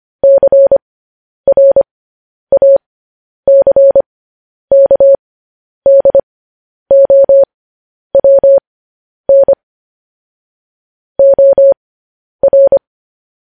Morse code, heard as a keyed tone, about 25 wpm, spaced as for 10 wpm, 550 Hz. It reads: CRACKDOWN OR